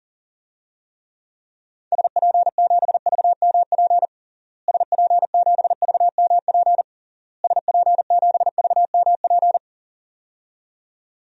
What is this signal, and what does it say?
Morse code, heard as a keyed tone, about 37 wpm, 700 Hz. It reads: SP7VMP SP7VMP SP7VMP
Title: DXCC-100 Call Signs - 6 Characters - Repeated 3x - 40wpm